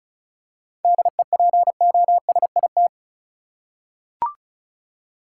Morse code, heard as a keyed tone, about 34 wpm, 700 Hz. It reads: DEPOSIT
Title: Seven Letter Words - 35wpm